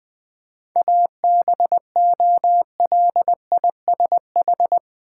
Morse code, heard as a keyed tone, about 21 wpm, 700 Hz. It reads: ABOLISH
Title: 7 Letter Words - First 3 Letters Given - 20wpm